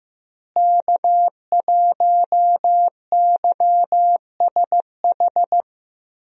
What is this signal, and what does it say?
Morse code, heard as a keyed tone, about 16 wpm, 700 Hz. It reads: K1YSH
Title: USA Call Signs - 5 Characters - Repeated 2x - 15wpm